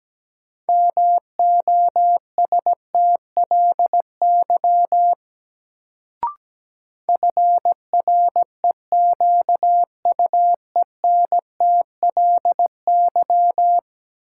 Morse code, heard as a keyed tone, about 17 wpm, 700 Hz. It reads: MOSTLY FREQUENTLY
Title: Sentences from Top 1300 Words - New Word Drill - 17wpm